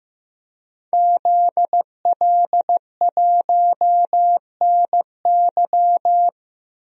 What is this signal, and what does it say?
Morse code, heard as a keyed tone, about 15 wpm, 700 Hz. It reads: ZL1NY
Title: DXCC-100 Call Signs - 5 Characters - Repeated 3x - 15wpm